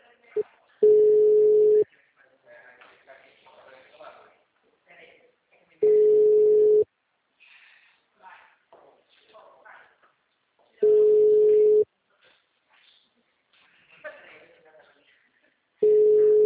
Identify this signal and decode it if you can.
Ringback tone (Europe, 425 Hz), with speech.